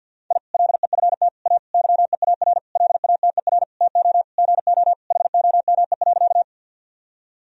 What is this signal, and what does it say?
Morse code, heard as a keyed tone, about 48 wpm, 700 Hz. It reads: I LEFT A YEAR LATER TO GO HOME.
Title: Sentences from Top 1000 Words - Review Part 4 - 50wpm